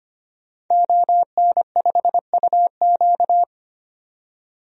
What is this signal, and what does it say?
Morse code, heard as a keyed tone, about 25 wpm, 700 Hz. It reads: ON5UQ
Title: DXCC-100 Call Signs - 5 Characters - 25wpm